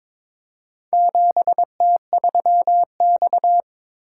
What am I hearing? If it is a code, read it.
Morse code, heard as a keyed tone, about 22 wpm, 700 Hz. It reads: 7T3X
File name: practice / 22wpm / Four Letter-Number Combinations - 22wpm